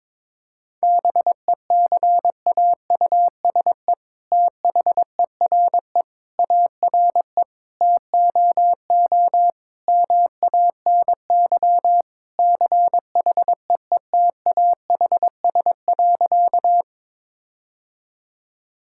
Morse code, heard as a keyed tone, about 22 wpm, 700 Hz. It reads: BECAUSE THERE ARE TOO MANY CHEETAHS.
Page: morse-code-ninja-practice-22wpm